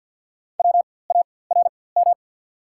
Morse code, heard as a keyed tone, about 54 wpm, spaced as for 24 wpm, 690 Hz. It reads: WARM
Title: 4x Character Spacing - Top 500 Words - 50wpm